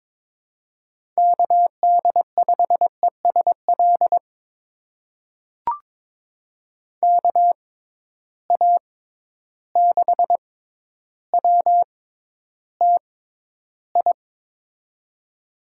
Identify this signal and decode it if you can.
Morse code, heard as a keyed tone, about 22 wpm, spaced as for 11 wpm, 700 Hz. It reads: KD5ESL KA6WTI
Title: USA Call Signs - 6 Characters - 6x Character Spacing - 22wpm